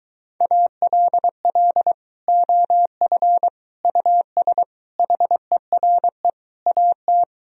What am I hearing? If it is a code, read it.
Morse code, heard as a keyed tone, about 23 wpm, 700 Hz. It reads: ALL OF US HERE AT